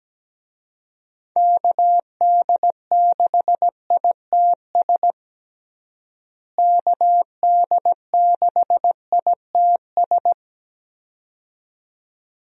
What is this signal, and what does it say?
Morse code, heard as a keyed tone, about 17 wpm, 700 Hz. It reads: KD6ITS KD6ITS
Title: USA Call Signs - 6 Characters - Repeated 2x - 17wpm